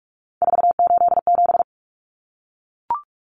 Morse code, heard as a keyed tone, about 44 wpm, 700 Hz. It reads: 487
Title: Common Cut Numbers - 3-Digit 45wpm